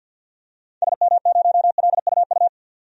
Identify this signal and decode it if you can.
Morse code, heard as a keyed tone, about 53 wpm, 700 Hz. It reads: SM0LUU
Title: DXCC-100 Call Signs - 6 Characters - 50wpm